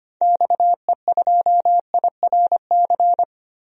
Morse code, heard as a keyed tone, about 25 wpm, 700 Hz. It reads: XE2IRC
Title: DXCC-100 Call Signs - 6 Characters - Repeated 3x - 25wpm